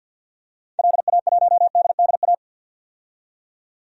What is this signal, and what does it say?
Morse code, heard as a keyed tone, about 52 wpm, 700 Hz. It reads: LA1DDA
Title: DXCC-100 Call Signs - 6 Characters - 50wpm